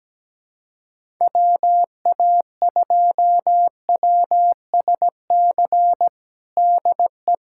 Morse code, heard as a keyed tone, about 17 wpm, 700 Hz. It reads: WA2WSC DE
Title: USA Callsign DE Callsign - 6 Characters - 17wpm